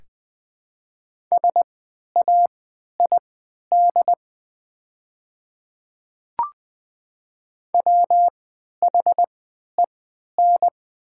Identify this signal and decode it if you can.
Morse code, heard as a keyed tone, about 20 wpm, spaced as for 11 wpm, 700 Hz. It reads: SAID WHEN
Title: 3x Character Spacing - Top 100 Words - 20wpm